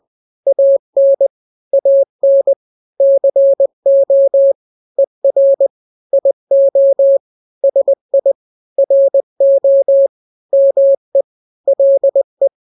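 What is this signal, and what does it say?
Morse code, heard as a keyed tone, about 20 wpm, 550 Hz. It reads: AN AN CO ER IO SI RO ME LE
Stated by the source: Morse drill of two-letter n-grams at 20/18 WPM (Farnsworth), from KK7RR.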